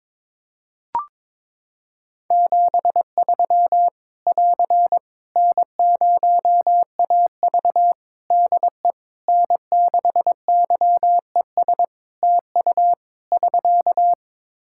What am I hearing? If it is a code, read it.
Morse code, heard as a keyed tone, about 22 wpm, 700 Hz. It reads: 73 <AR> N0AV DE N6YES TU <SK>
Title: Practice QSO - Exchange 4 - Ending - 22wpm